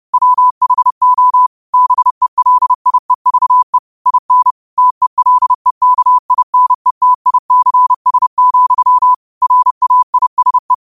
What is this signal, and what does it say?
Morse code, heard as a keyed tone, about 30 wpm, 1000 Hz. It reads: WHO BELIEVE IN TELEKINETICS, RAISE
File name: qotdfiles / 30wpm